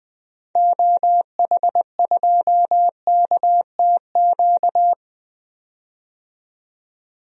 Morse code, heard as a keyed tone, about 21 wpm, 700 Hz. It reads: OH2KTQ
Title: DXCC-100 Call Signs - 6 Characters - 20wpm